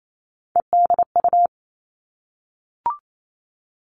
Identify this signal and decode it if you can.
Morse code, heard as a keyed tone, about 28 wpm, 700 Hz. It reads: EDU
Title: Three Challenging Letter-Number Combinations 28wpm